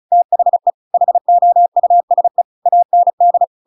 Morse code, heard as a keyed tone, about 34 wpm, 700 Hz. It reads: THE HOUSE AND
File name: Pinocchio / Pinocchio ch7 35 35 ik1hge